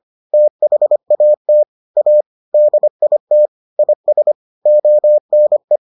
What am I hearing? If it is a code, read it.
Morse code, heard as a keyed tone, about 25 wpm, 600 Hz. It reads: THAT A DIT IS ONE